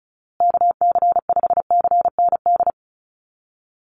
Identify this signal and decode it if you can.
Morse code, heard as a keyed tone, about 34 wpm, 700 Hz. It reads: KC5CND
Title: USA Call Signs Repeated 2x 35wpm